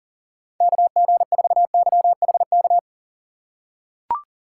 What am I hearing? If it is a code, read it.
Morse code, heard as a keyed tone, about 39 wpm, 700 Hz. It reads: KG4YHK
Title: USA Call Signs - 6 Characters - 6x Character Spacing - 40wpm